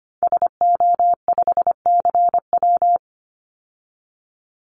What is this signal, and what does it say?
Morse code, heard as a keyed tone, about 25 wpm, 700 Hz. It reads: SO5CW
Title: Call Signs Repeated 2x - 25wpm